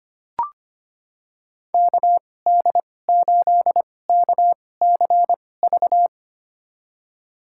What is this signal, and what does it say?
Morse code, heard as a keyed tone, about 25 wpm, spaced as for 18 wpm, 700 Hz. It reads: KD8KCV
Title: USA Call Signs - 2x Character Spacing - 25wpm